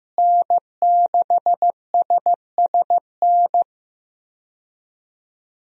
Morse code, heard as a keyed tone, about 15 wpm, 700 Hz. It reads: N6SSN
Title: USA Call Signs - 5 Characters - Repeated 3x - 15wpm